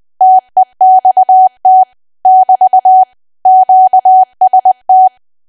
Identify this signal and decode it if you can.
Morse code, heard as a keyed tone, about 20 wpm, 750 Hz. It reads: TEXT <BT> QST
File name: cwa / W1AW-Level3 / 20WPM